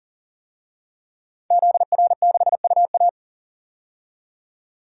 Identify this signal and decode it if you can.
Morse code, heard as a keyed tone, about 40 wpm, 700 Hz. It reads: ZR6UA
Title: DXCC-100 Call Signs - 5 Characters - 40wpm